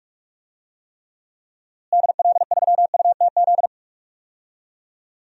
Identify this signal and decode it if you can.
Morse code, heard as a keyed tone, about 41 wpm, 700 Hz. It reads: DL3UTZ
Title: DXCC-100 Call Signs - 6 Characters - Repeated 3x - 45wpm